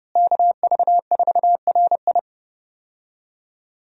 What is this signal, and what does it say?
Morse code, heard as a keyed tone, about 32 wpm, 700 Hz. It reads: KV4RI
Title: USA Call Signs - 5 Characters - Repeated 3x - 30wpm